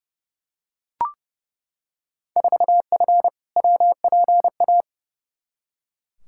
Morse code, heard as a keyed tone, about 31 wpm, 700 Hz. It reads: 4F WPA
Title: Field Day Exchanges - 30wpm